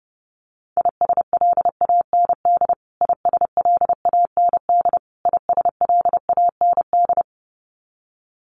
Morse code, heard as a keyed tone, about 30 wpm, 700 Hz. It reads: ISLAND ISLAND ISLAND
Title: Top 500 Words Repeated 30wpm